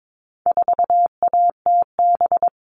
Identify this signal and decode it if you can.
Morse code, heard as a keyed tone, about 22 wpm, 700 Hz. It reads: 4ATB
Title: All Cut Numbers - 4-Digit 22wpm